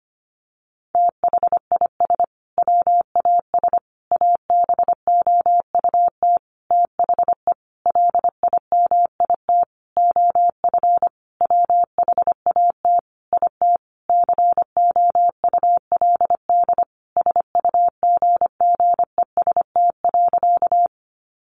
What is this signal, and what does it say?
Morse code, heard as a keyed tone, about 25 wpm, 700 Hz. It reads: THIS WAS ABOUT THE LIMIT OF WHAT IT COULD SUGGEST.
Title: Sentences from Top 1500 Words 25wpm